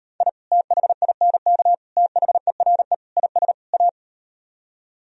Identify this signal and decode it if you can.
Morse code, heard as a keyed tone, about 38 wpm, 700 Hz. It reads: I THINK THERE IS A